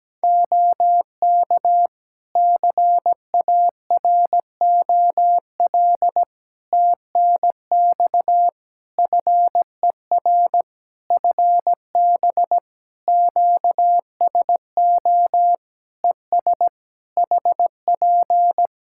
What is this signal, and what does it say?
Morse code, heard as a keyed tone, about 17 wpm, 700 Hz. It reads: OK CAROL TNX FER FB QSO ES HP
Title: Practice QSO - Exchange 4 - Ending - 17wpm